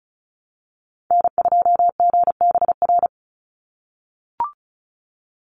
Morse code, heard as a keyed tone, about 34 wpm, 700 Hz. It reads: N2GBR
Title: NA SOTA Chaser Callsigns 35wpm